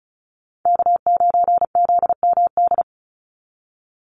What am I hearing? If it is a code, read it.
Morse code, heard as a keyed tone, about 36 wpm, 700 Hz. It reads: K9ZMD
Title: NA SOTA Chaser Callsigns 35wpm